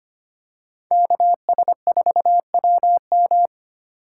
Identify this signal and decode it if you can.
Morse code, heard as a keyed tone, about 25 wpm, 700 Hz. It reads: KS4WM
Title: USA Call Signs - 5 Characters - 4x Character Spacing - 25wpm